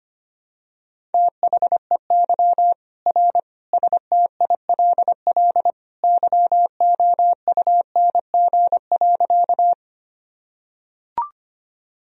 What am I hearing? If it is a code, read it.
Morse code, heard as a keyed tone, about 25 wpm, 700 Hz. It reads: THEY R STILL YOUNG.
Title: Sentences from Top 400 Words and Abbreviations 25wpm